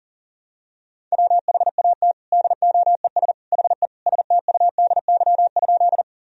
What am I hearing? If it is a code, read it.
Morse code, heard as a keyed tone, about 40 wpm, 700 Hz. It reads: WHAT DOES HE STUDY?
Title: Sentences from Top 1000 Words - Review Part 1 - 40wpm